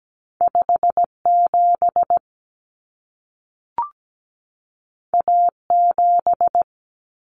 Morse code, heard as a keyed tone, about 17 wpm, 700 Hz. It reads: 57 A7
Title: Common Cut Numbers - 2-Digit 17wpm